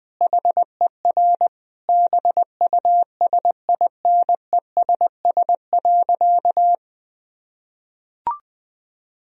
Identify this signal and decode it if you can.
Morse code, heard as a keyed tone, about 20 wpm, 700 Hz. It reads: HER BUSINESS.
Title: Sentences from Top 1000 Words - Review Part 1 - 20wpm